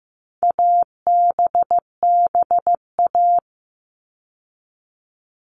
Morse code, heard as a keyed tone, about 15 wpm, 700 Hz. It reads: ABBA
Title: All Cut Numbers - 4-Digit 15wpm